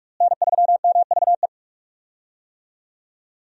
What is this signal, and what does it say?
Morse code, heard as a keyed tone, about 43 wpm, 700 Hz. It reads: N3MVE
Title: USA Call Signs - 5 Characters - Repeated 3x - 45wpm